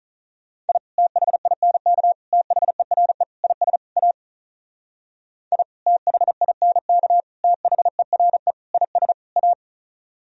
Morse code, heard as a keyed tone, about 43 wpm, 700 Hz. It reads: I THINK THERE IS A I THINK THERE IS A
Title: Speed-Racing - Sets of 5 Words 35wpm